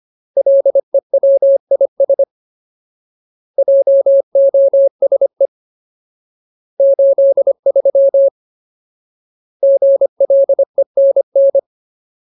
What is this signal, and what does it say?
Morse code, heard as a keyed tone, about 25 wpm, 550 Hz. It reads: LEWIS JOSE 83 GLENN
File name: Ultimate Word List / 25WPM Ultimate